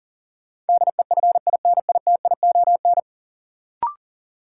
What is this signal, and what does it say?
Morse code, heard as a keyed tone, about 40 wpm, 700 Hz. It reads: DEFINITION
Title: Sentences from Top 1000 Words - Review Part 4 - New Word Drill - 40wpm